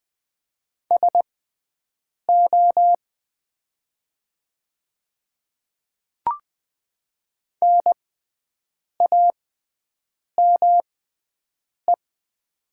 Morse code, heard as a keyed tone, about 20 wpm, spaced as for 7 wpm, 700 Hz. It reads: SO NAME
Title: 6x Character Spacing - Top 300 Words - 20wpm